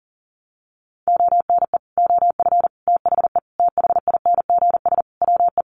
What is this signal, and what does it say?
Morse code, heard as a keyed tone, about 40 wpm, 700 Hz. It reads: ONE OF THE THINGS WE
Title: Sets of 5 Words Encore 40wpm